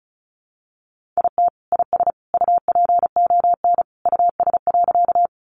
Morse code, heard as a keyed tone, about 34 wpm, 700 Hz. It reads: IT IS UPON US.
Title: Sentences from Top 1100 Words 35wpm